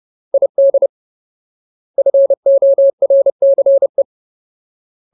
Morse code, heard as a keyed tone, about 30 wpm, 550 Hz. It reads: ID FORCE
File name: Ultimate Word List / 30WPM Ultimate